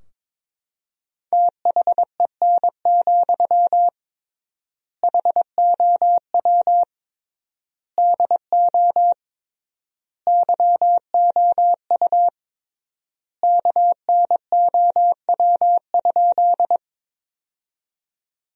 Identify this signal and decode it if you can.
Morse code, heard as a keyed tone, about 22 wpm, 700 Hz. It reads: THEN, HOW DO YOU KNOW?
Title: 3x Word-Spacing - Sentences from Top 100 Words - 22wpm